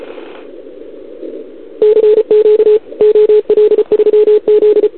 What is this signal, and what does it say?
Morse code, heard as a keyed tone, about 34 wpm, 440 Hz. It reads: CQ OL3Z
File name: CQWWCW2021 / OL3Z